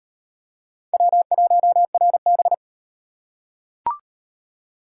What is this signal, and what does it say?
Morse code, heard as a keyed tone, about 38 wpm, 700 Hz. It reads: W1RB